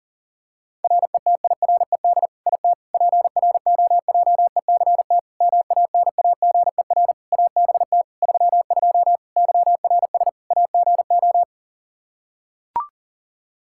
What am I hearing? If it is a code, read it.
Morse code, heard as a keyed tone, about 40 wpm, 700 Hz. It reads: RETIRED IT PROJECT MANAGER ABT 32 YRS AGO
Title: Practice QSO - Exchange 3 - More Info - 40wpm